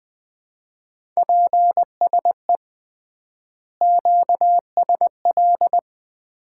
Morse code, heard as a keyed tone, about 20 wpm, 700 Hz. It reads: PSE QSL
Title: Common QSO Phrases - 3x Word-Spacing - 20wpm